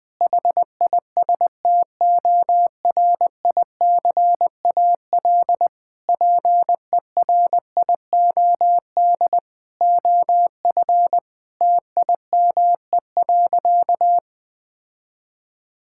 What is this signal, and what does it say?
Morse code, heard as a keyed tone, about 19 wpm, 700 Hz. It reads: HISTORICAL PERIOD OF TIME.
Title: Sentences from Top 1000 Words - Review Part 5 - 20wpm